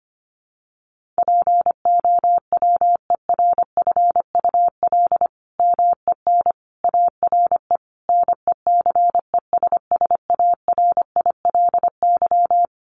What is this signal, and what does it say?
Morse code, heard as a keyed tone, about 25 wpm, 700 Hz. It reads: POWERFUL MEN ARE NECESSARILY
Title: Sentences from Top 1400 Words 25wpm